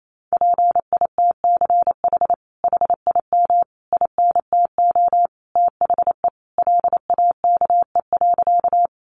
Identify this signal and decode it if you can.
Morse code, heard as a keyed tone, about 28 wpm, 700 Hz. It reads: PITCH HIM INTO THE LAKE.
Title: Sentences from Top 1300 Words 28wpm